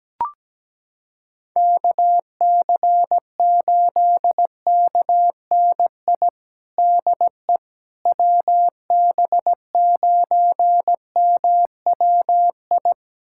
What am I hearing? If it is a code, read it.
Morse code, heard as a keyed tone, about 17 wpm, 700 Hz. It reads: KC8KNI DE WB9MWI
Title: USA Callsign DE Callsign 17wpm